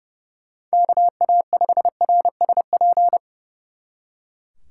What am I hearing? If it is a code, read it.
Morse code, heard as a keyed tone, about 37 wpm, 700 Hz. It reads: KA5RSP
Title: Speed Racing - USA Call Signs 30wpm